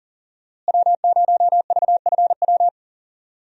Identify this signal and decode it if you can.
Morse code, heard as a keyed tone, about 40 wpm, 700 Hz. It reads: W0VFW
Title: USA Call Signs - 5 Characters - Repeated 2x - 40wpm